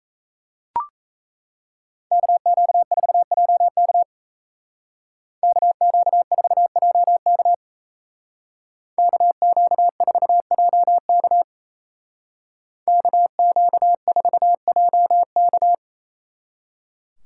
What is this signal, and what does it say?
Morse code, heard as a keyed tone, about 34 wpm, 700 Hz. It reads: KQ4JK KQ4JK KQ4JK KQ4JK
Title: Speed Racing - USA 2x2 Call Signs 28wpm